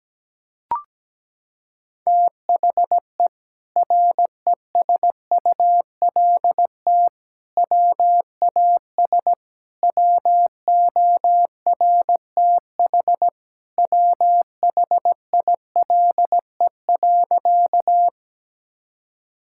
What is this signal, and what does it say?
Morse code, heard as a keyed tone, about 17 wpm, 700 Hz. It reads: THE RESULT WAS WORTH WHILE.
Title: Sentences from Top 1000 Words - Review Part 5 - 17wpm